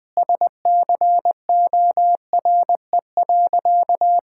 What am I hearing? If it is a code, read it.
Morse code, heard as a keyed tone, about 20 wpm, 700 Hz. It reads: SCORE.
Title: Sentences from Top 1600 Words 20wpm